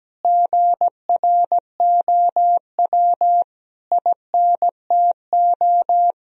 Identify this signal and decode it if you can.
Morse code, heard as a keyed tone, about 17 wpm, 700 Hz. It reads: GROW INTO
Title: Sentences from Top 1000 Words - Review Part 1 - 17wpm